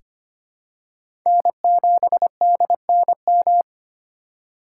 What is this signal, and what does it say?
Morse code, heard as a keyed tone, about 25 wpm, 700 Hz. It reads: N7DNM
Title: USA Call Signs - 5 Characters - 4x Character Spacing - 25wpm